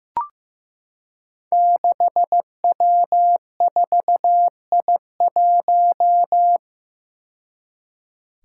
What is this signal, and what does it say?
Morse code, heard as a keyed tone, about 15 wpm, 700 Hz. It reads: 6W4I1
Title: Five Challenging Letter-Number Combinations 15wpm